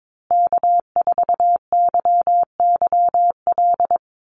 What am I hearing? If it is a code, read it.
Morse code, heard as a keyed tone, about 22 wpm, 700 Hz. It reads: K4YYL
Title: Call Signs Repeated 2x - 22wpm